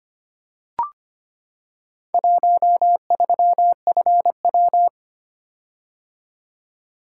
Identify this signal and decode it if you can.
Morse code, heard as a keyed tone, about 25 wpm, 700 Hz. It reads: 13FW
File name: practice / 25wpm / Four Letter-Number Combinations - 25wpm